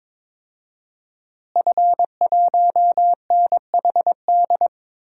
Morse code, heard as a keyed tone, about 22 wpm, 700 Hz. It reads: F1NHD
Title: DXCC-100 Call Signs - 5 Characters - Repeated 2x - 22wpm